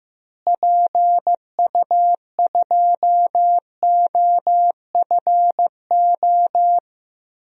Morse code, heard as a keyed tone, about 15 wpm, 700 Hz. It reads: PU2OFO
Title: DXCC-100 Call Signs - 6 Characters - Repeated 3x - 15wpm